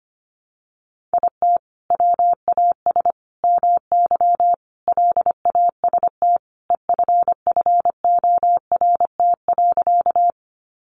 Morse code, heard as a keyed tone, about 25 wpm, 700 Hz. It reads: IT WAS MY LAST EFFORT.
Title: Sentences from Top 1200 Words 25wpm